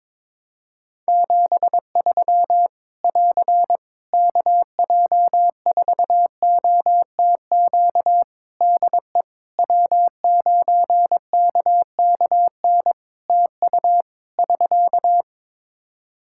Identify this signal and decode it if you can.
Morse code, heard as a keyed tone, about 22 wpm, 700 Hz. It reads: 73 <AR> KJ4OTQ DE W9KKN TU <SK>
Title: Practice QSO - Exchange 4 - Ending - 22wpm